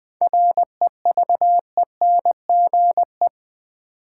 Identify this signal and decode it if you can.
Morse code, heard as a keyed tone, about 20 wpm, 700 Hz. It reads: REVENGE
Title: 7 Letter Words - First 3 Letters Given - 20wpm